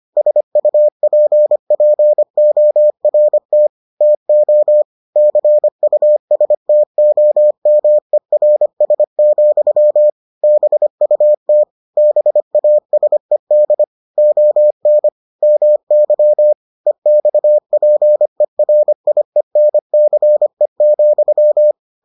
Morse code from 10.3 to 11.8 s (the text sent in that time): BUT